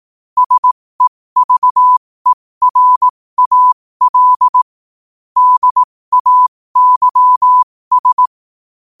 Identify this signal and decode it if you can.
Morse code, heard as a keyed tone, about 18 wpm, spaced as for 15 wpm, 1000 Hz. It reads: SEVERAL DAYS
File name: qotdfiles / 15wpm